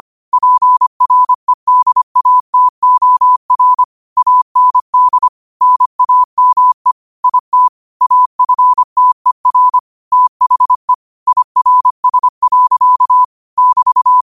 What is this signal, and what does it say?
Morse code, heard as a keyed tone, about 25 wpm, 1000 Hz. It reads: PREDATOR AND NAME IT AFTER THE IRS. <BT>